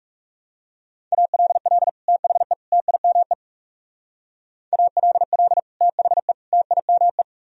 Morse code, heard as a keyed tone, about 37 wpm, 700 Hz. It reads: ALL THE TIME ALL THE TIME
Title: Speed-Racing - Sets of 3 Words 30wpm